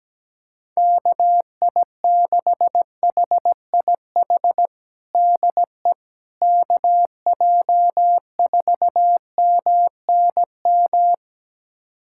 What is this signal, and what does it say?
Morse code, heard as a keyed tone, about 17 wpm, 700 Hz. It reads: KI6HIH DE KJ4MNM
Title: USA Callsign DE Callsign 17wpm